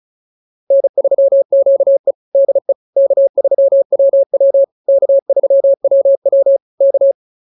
Morse code, heard as a keyed tone, about 34 wpm, 550 Hz. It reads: N3QE DE K3WW K3WW K
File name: QSO 101 35